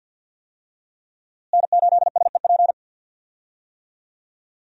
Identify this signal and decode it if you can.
Morse code, heard as a keyed tone, about 53 wpm, 700 Hz. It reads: N8SEP
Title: USA Call Signs - 5 Characters - Repeated 3x - 50wpm